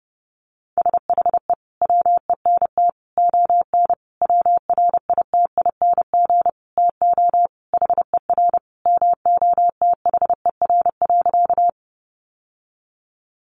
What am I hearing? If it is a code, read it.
Morse code, heard as a keyed tone, about 28 wpm, 700 Hz. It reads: SHE WENT ON WRITING TO HER MOTHER.
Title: Sentences from Top 800 Words 30wpm